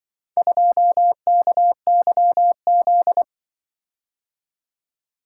Morse code, heard as a keyed tone, about 24 wpm, 700 Hz. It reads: 2KYZ
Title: Speed-Racing - Four Letter-Number Combinations - 20wpm